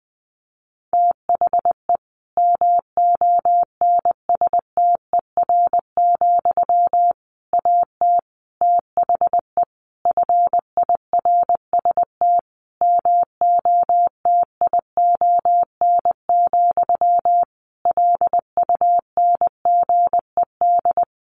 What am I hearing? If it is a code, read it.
Morse code, heard as a keyed tone, about 20 wpm, 700 Hz. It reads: THE MONSTER, AT THE FIRST MOTION, LUNGED
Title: A Sound Of Thunder 20wpm